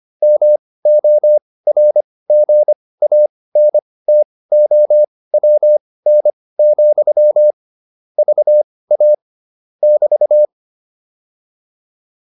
Morse code, heard as a keyed tone, about 25 wpm, spaced as for 18 wpm, 600 Hz. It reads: MORGANTOWN, VA <BT>
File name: QSO302 18